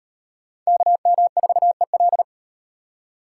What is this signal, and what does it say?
Morse code, heard as a keyed tone, about 38 wpm, 700 Hz. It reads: KM4EL